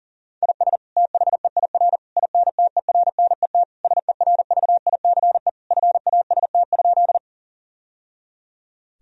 Morse code, heard as a keyed tone, about 40 wpm, 700 Hz. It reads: IS THEIR INTERNET SERVICE FAST?
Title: Sentences from Top 1000 Words - Review Part 4 - 40wpm